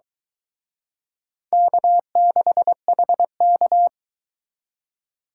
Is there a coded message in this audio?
Yes: Morse code from 1.5 to 3.9 s, heard as a keyed tone, about 22 wpm, 700 Hz. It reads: K6HK